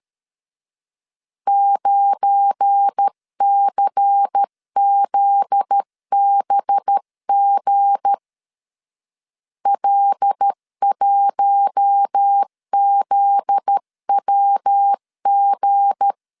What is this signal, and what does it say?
Morse code, heard as a keyed tone, about 13 wpm, 800 Hz. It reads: 9CZBG L1ZWG
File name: LEZIONI / 7 GF9Z / 7 GF9Z 12wpm 2147